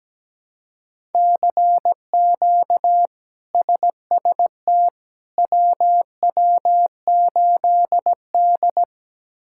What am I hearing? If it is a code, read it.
Morse code, heard as a keyed tone, about 17 wpm, 700 Hz. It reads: CQ SST WW8D
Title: K1USN SST QSO Practice - 17wpm